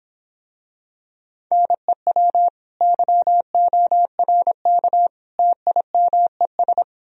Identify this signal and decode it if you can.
Morse code, heard as a keyed tone, about 26 wpm, 700 Hz. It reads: NEW YORK TIMES